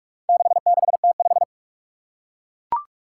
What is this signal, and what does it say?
Morse code, heard as a keyed tone, about 41 wpm, 700 Hz. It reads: 66T5
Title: Four Challenging Letter-Number Combinations 45wpm